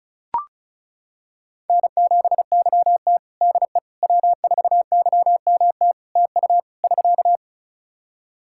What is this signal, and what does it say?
Morse code, heard as a keyed tone, about 34 wpm, 700 Hz. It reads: N7YT DE W4YMT TU <SK>
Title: K1USN SST QSO Practice 35wpm